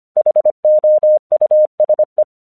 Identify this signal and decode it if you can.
Morse code, heard as a keyed tone, about 25 wpm, 610 Hz. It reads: HOUSE
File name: 1 English 25wpm / AQC15